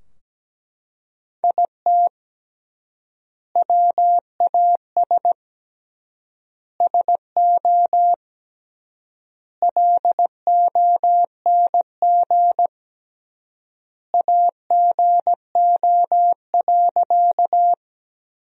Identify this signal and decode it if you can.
Morse code, heard as a keyed tone, about 17 wpm, 700 Hz. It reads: IT WAS SO LONG AGO.
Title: 3x Word-Spacing - Sentences from Top 500 Words - 17wpm